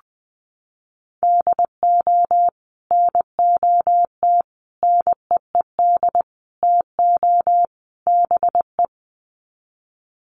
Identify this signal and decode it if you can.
Morse code, heard as a keyed tone, about 20 wpm, 700 Hz. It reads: DO NOT NEED TO BE
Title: Sets of 5 Words Encore 20wpm